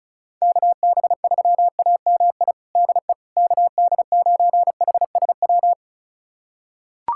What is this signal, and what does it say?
Morse code, heard as a keyed tone, about 36 wpm, 700 Hz. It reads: KB3AMI DE KD9HSW
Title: USA Callsign DE Callsign - 6 Characters - 35wpm